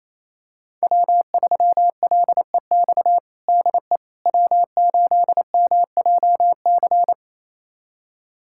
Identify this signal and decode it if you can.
Morse code, heard as a keyed tone, about 28 wpm, 700 Hz. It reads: W3LEX DE W8MJC
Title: USA Callsign DE Callsign - 5 Characters - 28wpm